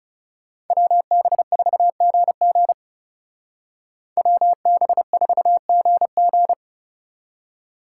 Morse code, heard as a keyed tone, about 37 wpm, 700 Hz. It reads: WB4GG WB4GG
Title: Speed Racing - USA 2x2 Call Signs 30wpm